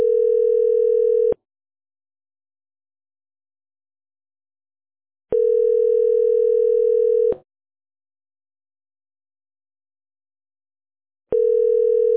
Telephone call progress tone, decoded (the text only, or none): ringback tone